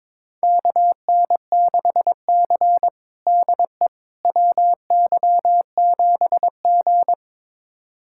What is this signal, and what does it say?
Morse code, heard as a keyed tone, about 22 wpm, 700 Hz. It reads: KN6C DE WY7G
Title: USA Callsign DE Callsign - 4 Characters - 22wpm